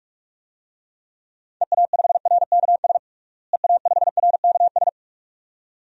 Morse code, heard as a keyed tone, about 41 wpm, 700 Hz. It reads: EA5RKS EA5RKS
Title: DXCC-100 Call Signs - 6 Characters - Repeated 2x - 45wpm